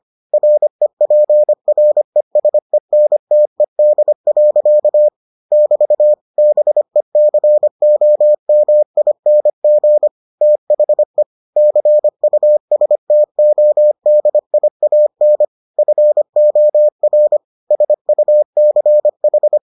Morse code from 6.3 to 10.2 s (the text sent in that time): BECOMING